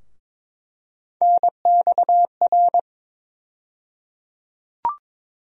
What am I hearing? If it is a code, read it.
Morse code, heard as a keyed tone, about 22 wpm, 700 Hz. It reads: NXR
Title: USA Call Sign Postfixes - 3 Letters 22wpm